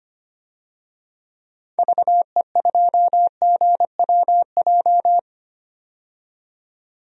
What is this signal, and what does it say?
Morse code, heard as a keyed tone, about 25 wpm, 700 Hz. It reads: VE2GWJ
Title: DXCC-100 Call Signs - 6 Characters - 25wpm